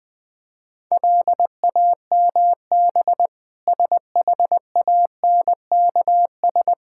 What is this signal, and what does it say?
Morse code, heard as a keyed tone, about 21 wpm, 700 Hz. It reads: LAMB SHANKS
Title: Difficult Words with Hints - 20wpm